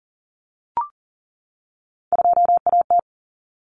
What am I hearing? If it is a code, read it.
Morse code, heard as a keyed tone, about 40 wpm, 700 Hz. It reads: 2AT
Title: Common Cut Numbers - 3-Digit 40wpm